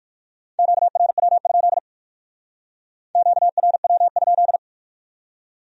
Morse code, heard as a keyed tone, about 56 wpm, 700 Hz. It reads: QRW? QRW?
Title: Speed-Racing Q-Codes - 45wpm